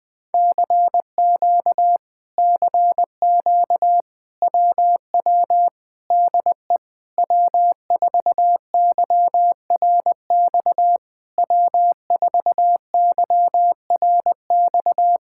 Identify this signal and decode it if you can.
Morse code, heard as a keyed tone, about 20 wpm, 700 Hz. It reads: CQ CQ WW DE W4YRX W4YRX
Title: Speed-Racing Common QSO Phrases Encore - 17wpm